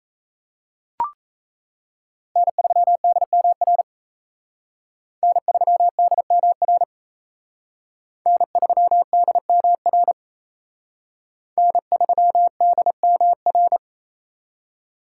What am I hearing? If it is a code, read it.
Morse code, heard as a keyed tone, about 34 wpm, 700 Hz. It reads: N3DMR N3DMR N3DMR N3DMR
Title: Speed Racing - USA 1x3 Call Signs 28wpm